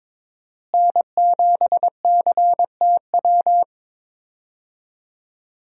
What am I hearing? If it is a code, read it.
Morse code, heard as a keyed tone, about 22 wpm, 700 Hz. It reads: N7CTW
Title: USA Call Signs - 5 Characters - Repeated 3x - 22wpm